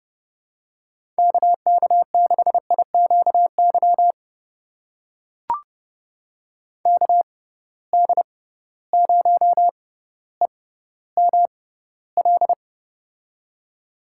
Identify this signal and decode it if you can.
Morse code, heard as a keyed tone, about 30 wpm, spaced as for 15 wpm, 700 Hz. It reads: KK6IQY KD0EML
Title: USA Call Signs - 6 Characters - 6x Character Spacing - 30wpm